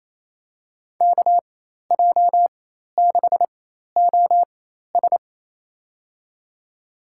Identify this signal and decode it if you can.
Morse code, heard as a keyed tone, about 27 wpm, spaced as for 13 wpm, 700 Hz. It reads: KJ6OS
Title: USA Call Signs - 5 Characters - 4x Character Spacing - 28wpm